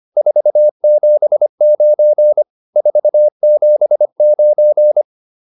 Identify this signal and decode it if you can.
Morse code, heard as a keyed tone, about 25 wpm, 600 Hz. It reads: 479 479